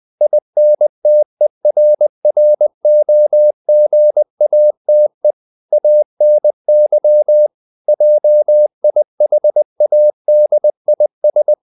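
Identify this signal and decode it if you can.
Morse code, heard as a keyed tone, about 20 wpm, 600 Hz. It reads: INTERROGATE ANY JIHADIS
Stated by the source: Morse code at 19 wpm from CW med Gnister, reading Ken Follett's novel Never in English.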